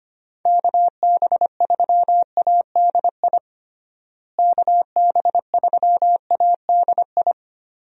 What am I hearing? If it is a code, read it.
Morse code, heard as a keyed tone, about 25 wpm, 700 Hz. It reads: KB3ADI KB3ADI
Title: USA Call Signs - 6 Characters - Repeated 2x - 25wpm